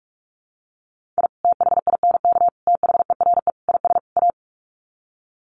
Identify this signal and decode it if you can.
Morse code, heard as a keyed tone, about 44 wpm, 700 Hz. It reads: I THINK THERE IS A
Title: Sets of 5 Words 45wpm